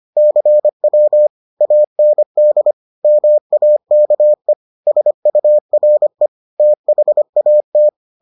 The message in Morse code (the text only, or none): CW AND MAKE SURE THAT